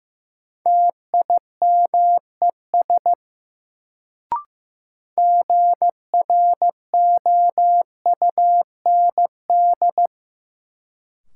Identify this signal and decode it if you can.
Morse code, heard as a keyed tone, about 15 wpm, 700 Hz. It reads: TIMES GROUND
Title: Sets of 3 Words - New Word Drill - 15wpm